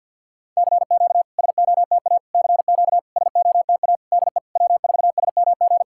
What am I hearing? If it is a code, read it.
Morse code, heard as a keyed tone, about 50 wpm, 690 Hz. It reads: CQ SOTA CQ SOTA DE W4SMG
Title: SOTA QSO Practice 50wpm